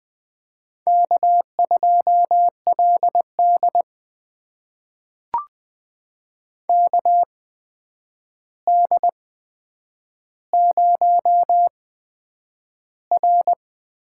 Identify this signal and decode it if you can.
Morse code, heard as a keyed tone, about 20 wpm, spaced as for 9 wpm, 700 Hz. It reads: K2LD KD0R
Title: USA Call Signs - 4 Characters - 8x Character Spacing - 20wpm